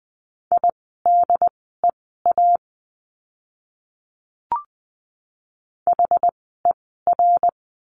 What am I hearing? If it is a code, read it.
Morse code, heard as a keyed tone, about 20 wpm, spaced as for 14 wpm, 700 Hz. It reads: IDEA HER
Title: 108 Lesson 18f - Letter m - Words 20wpm